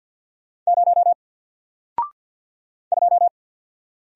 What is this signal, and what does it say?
Morse code, heard as a keyed tone, about 48 wpm, 700 Hz. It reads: 0 2
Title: Single Character (Letter:Number:Punctuation:Prosign) 50wpm